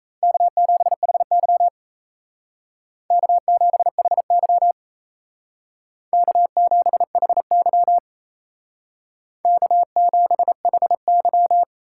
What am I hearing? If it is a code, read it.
Morse code, heard as a keyed tone, about 35 wpm, 700 Hz. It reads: K7HY K7HY K7HY K7HY
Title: Speed Racing - USA 1x2 Call Signs 28wpm